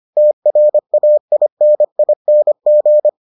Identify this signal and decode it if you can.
Morse code, heard as a keyed tone, about 25 wpm, 600 Hz. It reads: TRAINING